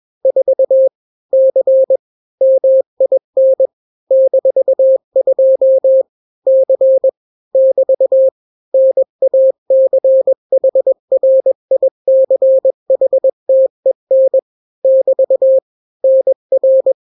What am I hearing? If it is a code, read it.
Morse code, heard as a keyed tone, about 21 wpm, 530 Hz. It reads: 4 C MIN -2 C <BT> NACHRICHTEN <BT> NR
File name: daily / 20wpm